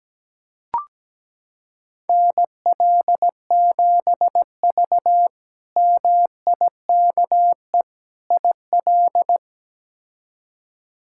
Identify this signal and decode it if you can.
Morse code, heard as a keyed tone, about 17 wpm, 700 Hz. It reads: NL7V MIKE IL
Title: K1USN SST QSO Practice - 17wpm